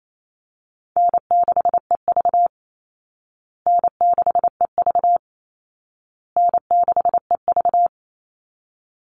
Morse code, heard as a keyed tone, about 28 wpm, 700 Hz. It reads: N6EV N6EV N6EV
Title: Easy Call Signs Repeated 3x - 28wpm